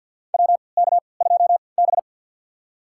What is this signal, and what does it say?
Morse code, heard as a keyed tone, about 47 wpm, spaced as for 28 wpm, 700 Hz. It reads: WK2B
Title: USA Call Signs - 4 Characters - 3x Character Spacing - 50wpm